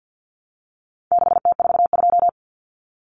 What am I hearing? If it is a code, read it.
Morse code, heard as a keyed tone, about 50 wpm, 700 Hz. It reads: 6T42
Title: Common Cut Numbers - 4-Digit 50wpm